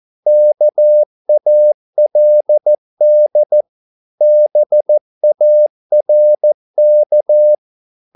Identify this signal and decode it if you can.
Morse code, heard as a keyed tone, about 14 wpm, 600 Hz. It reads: KALD BARK
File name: Korte ord DK 14wpm